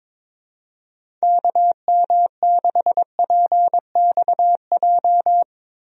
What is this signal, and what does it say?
Morse code, heard as a keyed tone, about 22 wpm, 700 Hz. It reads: KM6PXJ
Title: USA Call Signs - 6 Characters - Repeated 3x - 22wpm